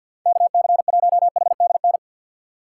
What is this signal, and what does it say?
Morse code, heard as a keyed tone, about 52 wpm, 700 Hz. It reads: KC1HDN
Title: USA Call Signs - 2x Character Spacing - 50wpm